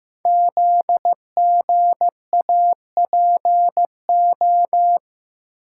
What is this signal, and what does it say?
Morse code, heard as a keyed tone, about 15 wpm, 700 Hz. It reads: ZGAPO
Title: Code Groups - 5 Letters - 15wpm